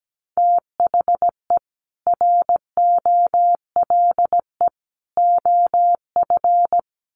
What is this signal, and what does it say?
Morse code, heard as a keyed tone, about 17 wpm, 700 Hz. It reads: THE ROLE OF
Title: Sets of 3 Words 17wpm